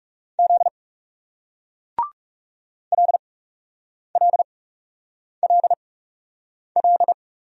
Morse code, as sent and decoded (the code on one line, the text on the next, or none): --.. / .-.. .-.. .-.. .-..
Z LLLL